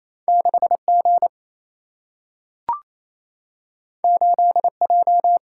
Morse code, heard as a keyed tone, about 28 wpm, 700 Hz. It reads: 6G 8J
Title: Single-Letter Single-Number 28wpm